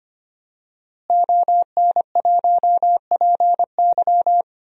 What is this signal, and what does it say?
Morse code, heard as a keyed tone, about 25 wpm, 700 Hz. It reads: ON1PY
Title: DXCC-100 Call Signs - 5 Characters - Repeated 2x - 25wpm